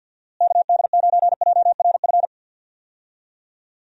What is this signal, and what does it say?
Morse code, heard as a keyed tone, about 51 wpm, 700 Hz. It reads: KD9JRF
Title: USA Call Signs - 6 Characters - Repeated 3x - 50wpm